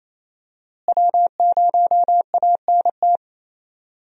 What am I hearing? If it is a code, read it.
Morse code, heard as a keyed tone, about 27 wpm, 700 Hz. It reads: W0ANT
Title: USA Call Signs - 5 Characters - Repeated 2x - 28wpm